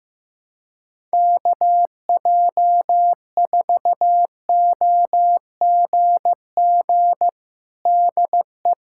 Morse code, heard as a keyed tone, about 16 wpm, 700 Hz. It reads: KJ4OGG DE
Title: USA Callsign DE Callsign - 6 Characters - 15wpm